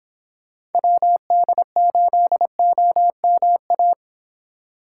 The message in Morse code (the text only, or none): WD8OMA